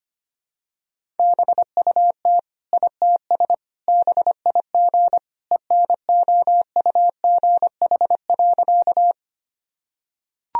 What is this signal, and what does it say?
Morse code, heard as a keyed tone, about 25 wpm, 700 Hz. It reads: BUT ITS BIG ENOUGH.
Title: Sentences from Top 1000 Words - Review Part 1 - 25wpm